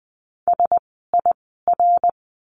Morse code, heard as a keyed tone, about 20 wpm, spaced as for 14 wpm, 700 Hz. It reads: SIR
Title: 061 Lesson 11f - Letter h - Words 20wpm